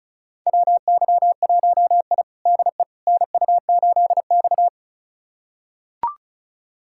Morse code, heard as a keyed tone, about 36 wpm, 700 Hz. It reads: WY1I DE NU8X
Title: USA Callsign DE Callsign - 4 Characters - 35wpm